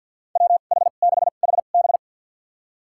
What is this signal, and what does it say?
Morse code, heard as a keyed tone, about 53 wpm, spaced as for 37 wpm, 700 Hz. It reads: WH6HB
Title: USA Call Signs - 2x Character Spacing - 50wpm